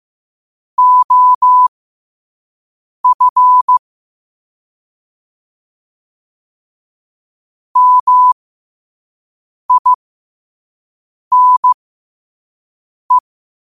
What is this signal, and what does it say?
Morse code, heard as a keyed tone, about 15 wpm, spaced as for 5 wpm, 1000 Hz. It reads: OF MINE
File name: qotdfiles / 5wpm